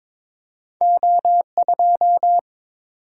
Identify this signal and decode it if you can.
Morse code, heard as a keyed tone, about 30 wpm, spaced as for 26 wpm, 700 Hz. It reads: O2
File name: Speed-Racing - Two Letter-Number Combinations - 22wpm